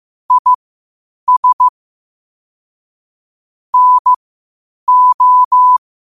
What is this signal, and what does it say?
Morse code, heard as a keyed tone, about 15 wpm, spaced as for 8 wpm, 1000 Hz. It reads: IS NO
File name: qotdfiles / 8wpm